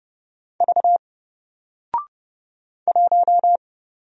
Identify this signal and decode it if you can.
Morse code, heard as a keyed tone, about 30 wpm, 700 Hz. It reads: V 1
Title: Single Letter-Number Alt 30wpm